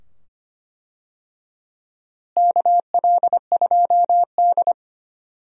Morse code, heard as a keyed tone, about 25 wpm, 700 Hz. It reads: KL2D
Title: USA Call Signs - 4 Characters - Repeated 2x - 25wpm